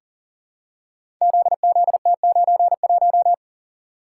Morse code, heard as a keyed tone, about 39 wpm, 700 Hz. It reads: ZZT91
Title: Code Groups - 5 Letter:Numbers - 40wpm